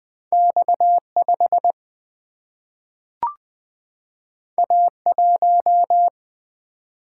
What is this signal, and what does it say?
Morse code, heard as a keyed tone, about 20 wpm, 700 Hz. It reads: X5 A1
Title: Single-Letter Single-Number 20wpm